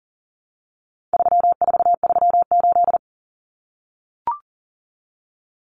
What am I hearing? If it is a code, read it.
Morse code, heard as a keyed tone, about 40 wpm, 700 Hz. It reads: 3438
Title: Numbers - 4-Digit 40wpm